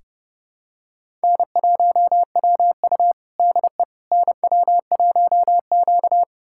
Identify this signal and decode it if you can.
Morse code, heard as a keyed tone, about 29 wpm, 700 Hz. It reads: N1WU DE NW1Q
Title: USA Callsign DE Callsign - 4 Characters - 30wpm